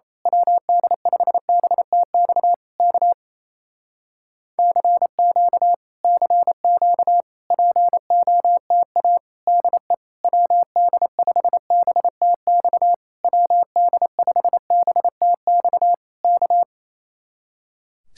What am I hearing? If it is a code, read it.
Morse code, heard as a keyed tone, about 26 wpm, 700 Hz. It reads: WD5BTX K CQ CQ POTA DE WD5BTX WD5BTX K
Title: Speed-Racing Common QSO Phrases Encore - 28wpm